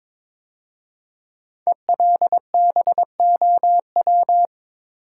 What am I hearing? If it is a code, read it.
Morse code, heard as a keyed tone, about 22 wpm, 690 Hz. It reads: ELBOW
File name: 5 Letter Words - First 3 Letters Given - 22wpm